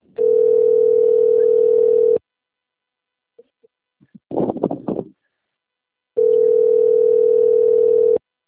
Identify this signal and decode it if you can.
Ringback tone (North America), 440 + 480 Hz.